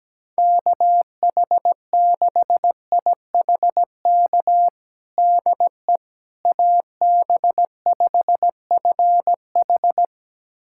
Morse code, heard as a keyed tone, about 17 wpm, 700 Hz. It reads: KH6IHK DE AB5FH
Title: USA Callsign DE Callsign 17wpm